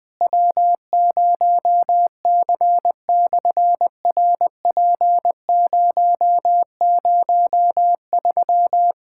Morse code, heard as a keyed tone, about 20 wpm, 700 Hz. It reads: W0C/RP003
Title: SOTA QSO Practice 20wpm